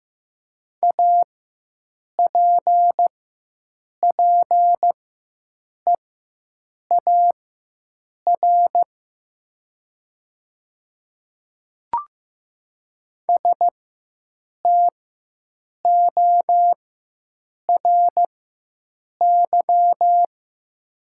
Morse code, heard as a keyed tone, about 15 wpm, spaced as for 7 wpm, 700 Hz. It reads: APPEAR STORY
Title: 4x Character Spacing - Top 500 Words - 15wpm